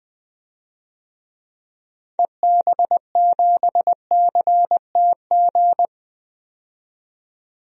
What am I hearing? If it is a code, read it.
Morse code, heard as a keyed tone, about 21 wpm, 700 Hz. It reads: EB7CTG
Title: DXCC-100 Call Signs - 6 Characters - 20wpm